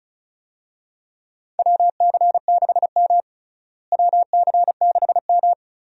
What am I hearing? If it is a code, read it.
Morse code, heard as a keyed tone, about 36 wpm, 700 Hz. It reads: WC6M WC6M
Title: USA Call Signs - 4 Characters - Repeated 2x - 35wpm